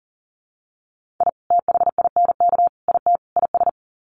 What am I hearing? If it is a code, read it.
Morse code, heard as a keyed tone, about 40 wpm, 700 Hz. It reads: I THINK IT IS
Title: Sets of 4 Words 40wpm